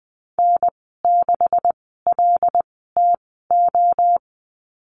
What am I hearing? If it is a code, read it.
Morse code, heard as a keyed tone, about 20 wpm, spaced as for 14 wpm, 700 Hz. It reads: N6LTO
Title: 140 Lesson 23g - Letter f - Call Signs 20wpm